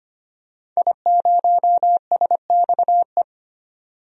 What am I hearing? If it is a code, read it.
Morse code, heard as a keyed tone, about 25 wpm, 700 Hz. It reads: I0SXE
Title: DXCC-100 Call Signs - 5 Characters - 25wpm